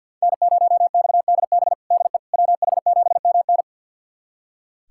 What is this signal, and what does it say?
Morse code, heard as a keyed tone, about 47 wpm, 700 Hz. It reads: N0XDB DE WH7MN
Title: USA Callsign DE Callsign - 5 Characters - 50wpm